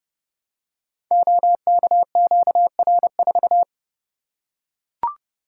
Morse code, heard as a keyed tone, about 30 wpm, 700 Hz. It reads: OKQR4
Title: Code Groups - 5 Letter:Numbers - 30wpm